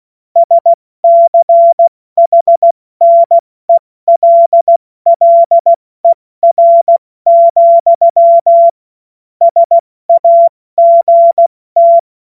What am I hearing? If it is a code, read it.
Morse code, heard as a keyed tone, about 16 wpm, 680 Hz. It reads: SCHNELLER, SAGT
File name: Morsekurs / Witze / Witz 20. Januar 2026 70 F80